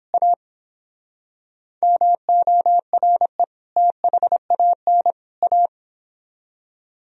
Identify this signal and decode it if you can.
Morse code, heard as a keyed tone, about 26 wpm, 700 Hz. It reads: A MORE THAN A
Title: Speed-Racing - Sets of 3 Words 22wpm